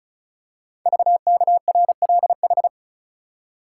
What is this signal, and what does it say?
Morse code, heard as a keyed tone, about 36 wpm, 700 Hz. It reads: VKRLH
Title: Code Groups - 5 Letters - 35wpm